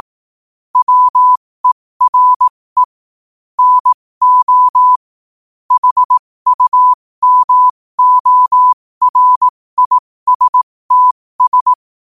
Morse code, heard as a keyed tone, about 18 wpm, spaced as for 15 wpm, 1000 Hz. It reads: WERE NO HUMORISTS
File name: qotdfiles / 15wpm